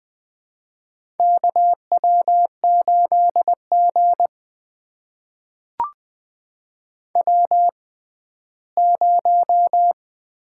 Morse code, heard as a keyed tone, about 21 wpm, spaced as for 14 wpm, 700 Hz. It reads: KW8G W0
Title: USA Call Signs - 4 Characters - 6x Character Spacing - 20wpm